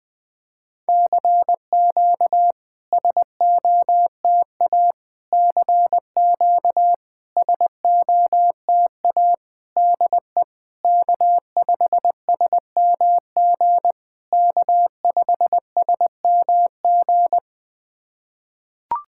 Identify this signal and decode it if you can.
Morse code, heard as a keyed tone, about 20 wpm, 700 Hz. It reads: CQ SOTA CQ SOTA DE K5SMG K5SMG
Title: SOTA QSO Practice 20wpm